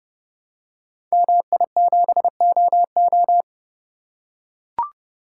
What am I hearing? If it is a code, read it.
Morse code, heard as a keyed tone, about 30 wpm, 700 Hz. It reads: MI7OO
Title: Code Groups - 5 Letter:Numbers - 30wpm